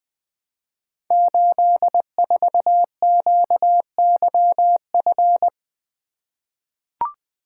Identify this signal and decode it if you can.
Morse code, heard as a keyed tone, about 20 wpm, 700 Hz. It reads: 84QYF
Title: Code Groups - 5 Letter:Numbers - 20wpm